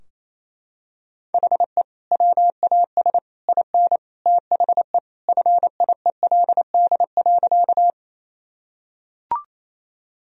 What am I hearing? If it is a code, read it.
Morse code, heard as a keyed tone, about 28 wpm, 700 Hz. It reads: HE WAS IN THE FIELD.
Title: Sentences from Top 1000 Words - Review Part 2 - 28wpm